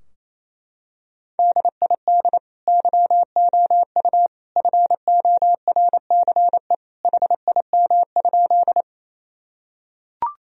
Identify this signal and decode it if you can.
Morse code, heard as a keyed tone, about 28 wpm, 700 Hz. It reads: DID YOU FORCE HIM?
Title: Sentences from Top 1000 Words - Review Part 3 - 28wpm